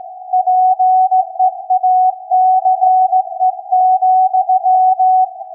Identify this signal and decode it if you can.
Morse code, heard as a keyed tone, about 16 wpm, 740 Hz. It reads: PEACE,
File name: iz7slz / VLF / SAQ 16-11-2022